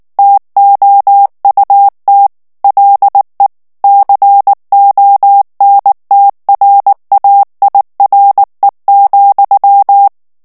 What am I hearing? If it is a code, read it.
Morse code, heard as a keyed tone, about 19 wpm, 800 Hz. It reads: TOUT LE CONTRAIRE,
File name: FRANCAIS TEXTE CLAIR 2 VITESSE 18 MOTS MINUTES